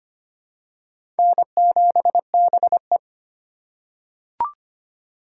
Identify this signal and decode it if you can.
Morse code, heard as a keyed tone, about 25 wpm, 700 Hz. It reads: N7BE
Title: USA Call Signs - 4 Characters - 2x Character Spacing - 25wpm